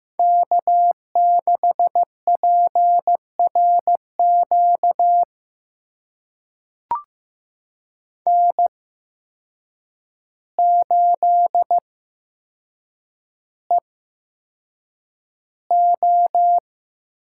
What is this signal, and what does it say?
Morse code, heard as a keyed tone, about 15 wpm, spaced as for 7 wpm, 700 Hz. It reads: K6PRQ N8EO
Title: USA Call Signs - 5 Characters - 8x Character Spacing - 15wpm